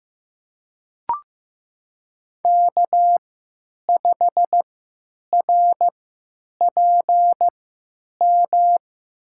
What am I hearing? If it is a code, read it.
Morse code, heard as a keyed tone, about 15 wpm, spaced as for 9 wpm, 700 Hz. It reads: K5RPM
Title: USA Call Signs - 5 Characters - 3x Character Spacing - 15wpm